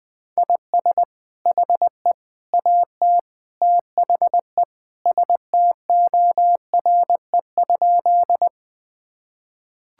Morse code, heard as a keyed tone, about 21 wpm, 700 Hz. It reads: IS HE AT THE STORE?
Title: Sentences from Top 1000 Words - Review Part 5 - 20wpm